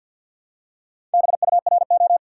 Morse code, heard as a keyed tone, about 54 wpm, 700 Hz. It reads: BURO
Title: Top 90 QSO Elements 50wpm